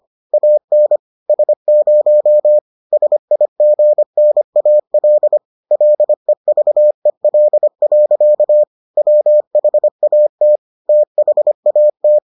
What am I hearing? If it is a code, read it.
Morse code, heard as a keyed tone, about 25 wpm, 600 Hz. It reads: AN S0 SIGNAL LEVEL. WHAT THAT